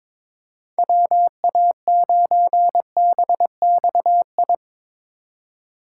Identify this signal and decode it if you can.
Morse code, heard as a keyed tone, about 22 wpm, 700 Hz. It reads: WA9BXI
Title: USA Call Signs - 6 Characters - 2x Character Spacing - 22wpm